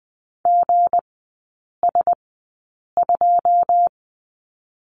Morse code, heard as a keyed tone, about 20 wpm, spaced as for 8 wpm, 690 Hz. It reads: GS2
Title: 159 Lesson 26e - Letter g - Three Characters 20wpm